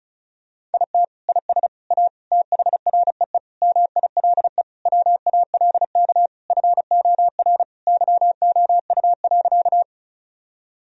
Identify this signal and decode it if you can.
Morse code, heard as a keyed tone, about 36 wpm, 700 Hz. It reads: IT IS A THREE MILE WALK FOR YOU.
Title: Sentences from Top 1000 Words - Review Part 1 - 35wpm